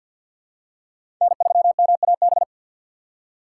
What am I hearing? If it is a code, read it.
Morse code, heard as a keyed tone, about 51 wpm, 690 Hz. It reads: N3MAB
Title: USA Call Signs - 5 Characters - 6x Character Spacing - 50wpm